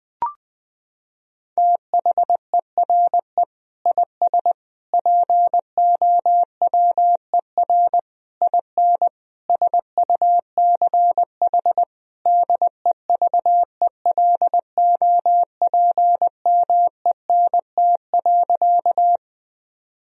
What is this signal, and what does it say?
Morse code, heard as a keyed tone, about 21 wpm, 700 Hz. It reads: THERE IS POWER IN SUCH DEVELOPMENT.
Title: Sentences from Top 1000 Words - Review Part 4 - 20wpm